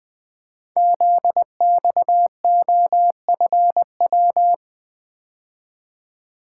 Morse code, heard as a keyed tone, about 20 wpm, 700 Hz. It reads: ZXOFW
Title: Code Groups - 5 Letter:Numbers - 20wpm